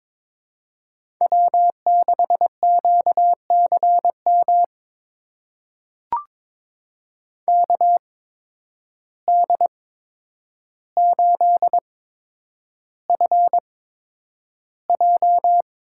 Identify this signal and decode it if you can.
Morse code, heard as a keyed tone, about 22 wpm, spaced as for 9 wpm, 700 Hz. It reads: W6QCM KD8FJ
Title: USA Call Signs - 5 Characters - 8x Character Spacing - 22wpm